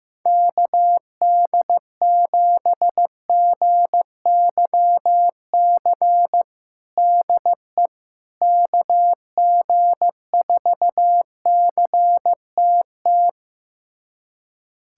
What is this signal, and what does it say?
Morse code, heard as a keyed tone, about 15 wpm, 700 Hz. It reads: KD7GYC DE KG4CTT
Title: USA Callsign DE Callsign - 6 Characters - 15wpm